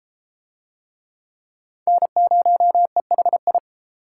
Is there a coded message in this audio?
Yes: Morse code from 1.9 to 3.6 s, heard as a keyed tone, about 32 wpm, 700 Hz. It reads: N0EHI